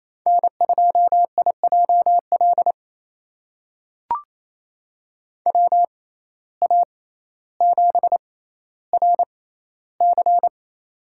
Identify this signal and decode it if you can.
Morse code, heard as a keyed tone, about 28 wpm, spaced as for 14 wpm, 700 Hz. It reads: N2IJL WA7RC
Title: USA Call Signs - 5 Characters - 6x Character Spacing - 28wpm